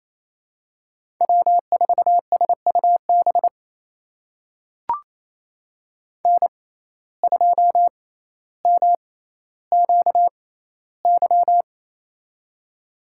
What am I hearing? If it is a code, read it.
Morse code, heard as a keyed tone, about 28 wpm, spaced as for 14 wpm, 700 Hz. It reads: W4SUB N2MQY
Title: USA Call Signs - 5 Characters - 6x Character Spacing - 28wpm